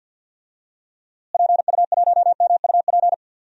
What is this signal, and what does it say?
Morse code, heard as a keyed tone, about 47 wpm, 700 Hz. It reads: PU1MUP
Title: DXCC-100 Call Signs - 6 Characters - Repeated 2x - 50wpm